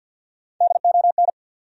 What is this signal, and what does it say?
Morse code, heard as a keyed tone, about 50 wpm, 700 Hz. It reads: DON
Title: USA Common Names - 50wpm